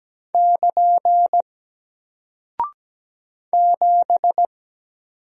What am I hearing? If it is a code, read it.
Morse code, heard as a keyed tone, about 17 wpm, 700 Hz. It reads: <KN> 7
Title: Single Character (Letter:Number:Punctuation:Prosign) Alt 17wpm